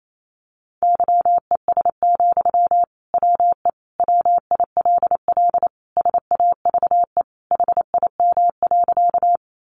Morse code, heard as a keyed tone, about 28 wpm, 700 Hz. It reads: YES, WE WILL SAVE HIM.
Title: Sentences from Top 1100 Words 28wpm